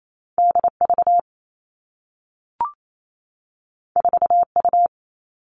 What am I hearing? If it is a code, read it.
Morse code, heard as a keyed tone, about 28 wpm, 700 Hz. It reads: D V 4 U
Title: Two Challenging Letter-Number Combinations 28wpm